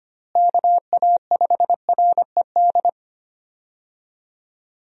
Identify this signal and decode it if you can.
Morse code, heard as a keyed tone, about 25 wpm, 700 Hz. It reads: KA5RED
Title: USA Call Signs - 6 Characters - Repeated 3x - 25wpm